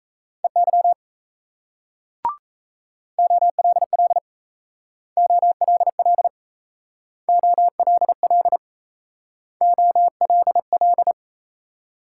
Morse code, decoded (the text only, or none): EY OLL OLL OLL OLL